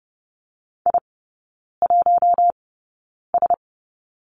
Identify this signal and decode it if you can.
Morse code, heard as a keyed tone, about 30 wpm, spaced as for 9 wpm, 700 Hz. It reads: I1S
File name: Lesson 18e - Letter m - Three Characters 30wpm